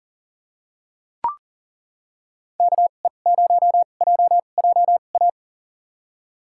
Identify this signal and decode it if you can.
Morse code, heard as a keyed tone, about 40 wpm, spaced as for 29 wpm, 700 Hz. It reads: KE0JJA
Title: USA Call Signs - 6 Characters - 2x Character Spacing - 40wpm